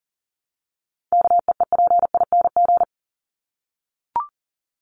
Morse code, heard as a keyed tone, about 40 wpm, 700 Hz. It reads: KEEPING
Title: ING Suffix - Top 100 Words with Base Words 40wpm